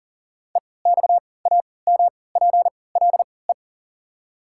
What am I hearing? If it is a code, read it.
Morse code, heard as a keyed tone, about 40 wpm, spaced as for 23 wpm, 700 Hz. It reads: EXAMPLE
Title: 3x Character Spacing - Top 300 Words - 40wpm